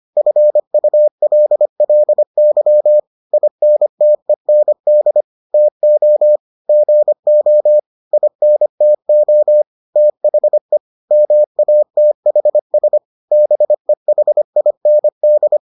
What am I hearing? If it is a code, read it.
Morse code, heard as a keyed tone, about 25 wpm, 600 Hz. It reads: FULLY INTEND TO GO INTO THE MATHS BEHIND